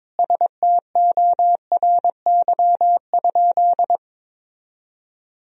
Morse code, heard as a keyed tone, about 22 wpm, 700 Hz. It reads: STORY?
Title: Sentences from Top 1000 Words - Review Part 1 - 22wpm